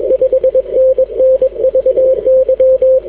Morse code, heard as a keyed tone, about 22 wpm, 520 Hz. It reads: 5NN VY
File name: w5un 160m